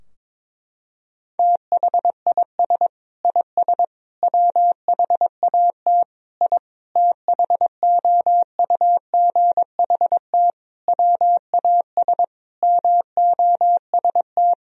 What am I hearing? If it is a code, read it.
Morse code, heard as a keyed tone, about 22 wpm, 700 Hz. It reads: THIS IS WHAT I THOUGHT WAS MOST
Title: Sentences from Top 1000 Words - Review Part 4 - 22wpm